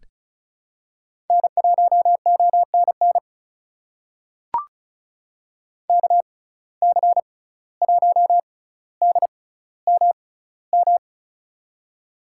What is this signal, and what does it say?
Morse code, heard as a keyed tone, about 36 wpm, spaced as for 17 wpm, 700 Hz. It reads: N1ONN KC1DMM
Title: USA Call Signs - 6x Character Spacing - 35wpm